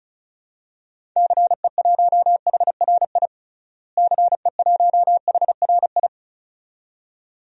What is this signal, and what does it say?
Morse code, heard as a keyed tone, about 34 wpm, 700 Hz. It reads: CE1HRI CE1HRI
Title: DXCC-100 Call Signs - 6 Characters - Repeated 2x - 35wpm